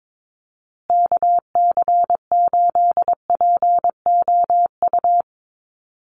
Morse code, heard as a keyed tone, about 22 wpm, 700 Hz. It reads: KC8POU
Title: USA Call Signs Repeated 3x 22wpm